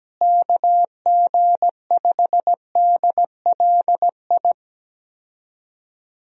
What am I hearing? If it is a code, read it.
Morse code, heard as a keyed tone, about 17 wpm, 700 Hz. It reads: KG5DLI
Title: USA Call Signs - 6 Characters - Repeated 3x - 17wpm